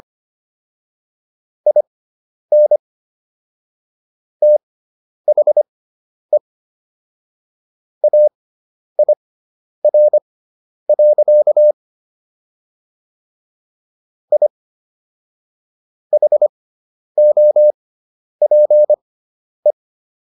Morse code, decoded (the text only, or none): IN THE AIR. I HOPE